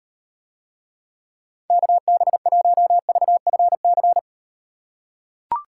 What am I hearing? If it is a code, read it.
Morse code, heard as a keyed tone, about 38 wpm, 700 Hz. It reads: KB1VFC